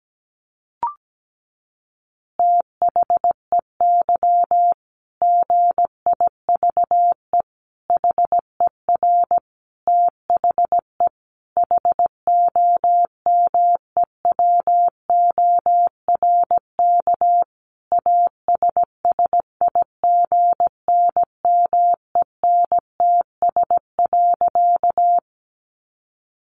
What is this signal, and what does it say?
Morse code, heard as a keyed tone, about 17 wpm, 700 Hz. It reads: THEY GIVE HER THE HOMEWORK ASSIGNMENTS.
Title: Easy Phrases 17wpm